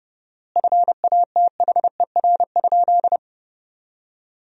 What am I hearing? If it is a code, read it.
Morse code, heard as a keyed tone, about 30 wpm, 700 Hz. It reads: FATHER?
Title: 4x Word-Spacing - Sentences from Top 400 Words - 30wpm